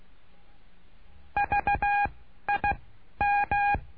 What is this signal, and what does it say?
Morse code, heard as a keyed tone, about 16 wpm, 820 Hz. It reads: V I M
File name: cursos / CURSO DE TELEGRAFIA / Telegrafia - Trilha 09